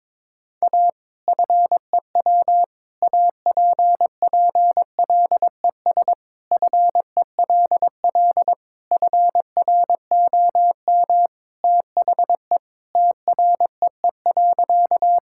Morse code, heard as a keyed tone, about 22 wpm, 700 Hz. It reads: A FEW APPLES FELL FROM THE TREE.
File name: practice / 22wpm / Sentences with 2-Word Phrases - 22wpm